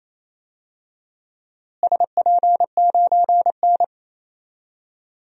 Morse code, heard as a keyed tone, about 28 wpm, 700 Hz. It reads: SP9N
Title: DXCC-100 Call Signs - 4 Characters - Repeated 2x - 28wpm